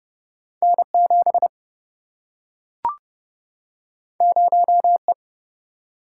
Morse code, heard as a keyed tone, about 30 wpm, 700 Hz. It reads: N 7 0 E
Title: Single-Letter Single-Number 30wpm